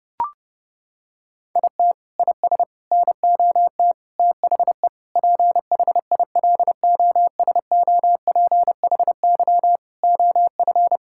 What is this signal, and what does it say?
Morse code, heard as a keyed tone, about 30 wpm, 700 Hz. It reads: IT IS NOT THE PHILOSOPHY OF
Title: Sentences from Top 1000 Words - Review Part 6 - 30wpm